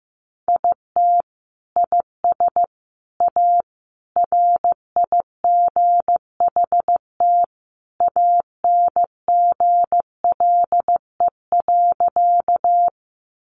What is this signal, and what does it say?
Morse code, heard as a keyed tone, about 15 wpm, 700 Hz. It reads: IT IS A RIGHT ANGLE.
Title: Sentences from Top 1500 Words 15wpm